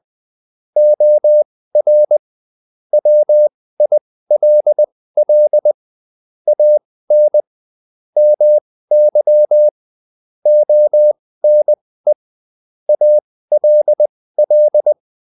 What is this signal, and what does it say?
Morse code, heard as a keyed tone, about 20 wpm, spaced as for 15 wpm, 600 Hz. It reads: OR WILL AN MY ONE ALL
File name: Top-100-Words-15